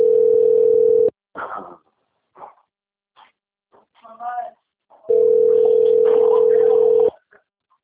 Ringback tone (North America), 440 + 480 Hz.
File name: out- - - - - .366